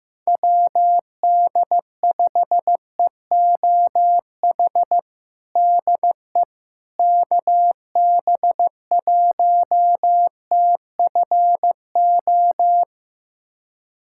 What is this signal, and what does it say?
Morse code, heard as a keyed tone, about 15 wpm, 700 Hz. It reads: WD5EOH DE KB1TFO
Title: USA Callsign DE Callsign - 6 Characters - 15wpm